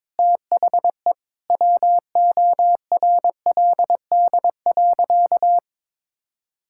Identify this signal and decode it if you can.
Morse code, heard as a keyed tone, about 22 wpm, 700 Hz. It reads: THE WORLD.
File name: practice / 22wpm / Sentences with 8-Word Phrases - 22wpm